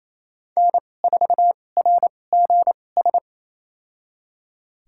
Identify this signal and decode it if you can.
Morse code, heard as a keyed tone, about 28 wpm, spaced as for 20 wpm, 700 Hz. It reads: N4RGS
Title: USA Call Signs - 5 Characters - 2x Character Spacing - 28wpm